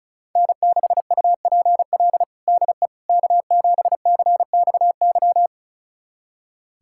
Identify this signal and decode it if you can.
Morse code, heard as a keyed tone, about 34 wpm, 700 Hz. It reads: N6UPL DE K7CXY
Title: USA Callsign DE Callsign - 5 Characters - 35wpm